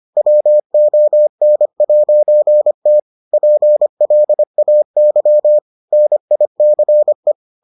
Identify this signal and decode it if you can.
Morse code, heard as a keyed tone, about 25 wpm, 600 Hz. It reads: WON'T PLAY NICE